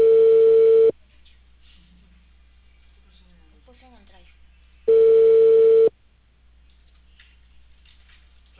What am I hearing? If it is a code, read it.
Ringback tone (China, 450 Hz), with speech.